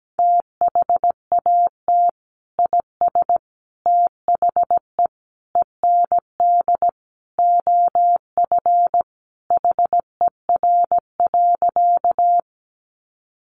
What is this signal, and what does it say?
Morse code, heard as a keyed tone, about 17 wpm, 700 Hz. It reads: THAT IS THE END OF HER.
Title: Sentences from Top 200 Words 17wpm